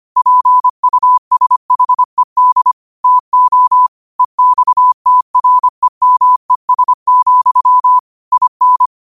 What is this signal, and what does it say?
Morse code, heard as a keyed tone, about 25 wpm, 1000 Hz. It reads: PUSHED TO EXTREMES, IN
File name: qotdfiles / 25wpm